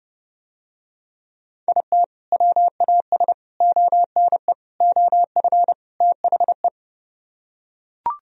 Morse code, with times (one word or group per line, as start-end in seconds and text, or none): 1.7-2.0 IT
2.3-3.3 WAS
3.6-4.5 ONE
4.8-5.7 OF
6.0-6.7 THE